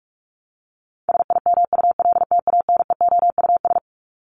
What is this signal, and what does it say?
Morse code, heard as a keyed tone, about 44 wpm, 700 Hz. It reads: SIMULTANEOUS
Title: LY Suffix - Top 100 Words with Base Words 45wpm